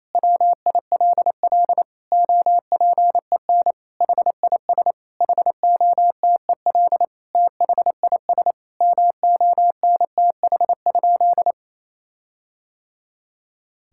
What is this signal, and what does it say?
Morse code, heard as a keyed tone, about 28 wpm, 700 Hz. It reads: WILL OPEN HIS HOTEL THIS MONTH?
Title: Sentences from Top 1600 Words 28wpm